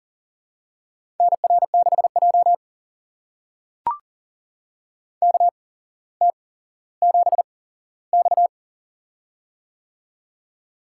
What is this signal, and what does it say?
Morse code, heard as a keyed tone, about 40 wpm, spaced as for 17 wpm, 700 Hz. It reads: NR6J KT7X
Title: USA Call Signs - 4 Characters - 8x Character Spacing - 40wpm